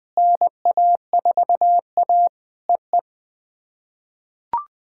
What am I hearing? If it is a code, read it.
Morse code, heard as a keyed tone, about 20 wpm, 700 Hz. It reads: NA4A EE
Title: POTA QSO Practice - 20wpm